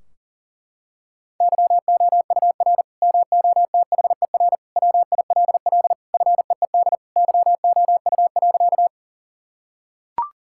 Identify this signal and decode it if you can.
Morse code, heard as a keyed tone, about 37 wpm, 700 Hz. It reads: YOUR MOTHER WILL FEED YOU.
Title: Sentences from Top 1600 Words 40wpm